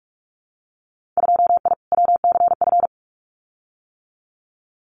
Morse code, heard as a keyed tone, about 46 wpm, 700 Hz. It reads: 2I WCF
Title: Winter Field Day Exchanges 45wpm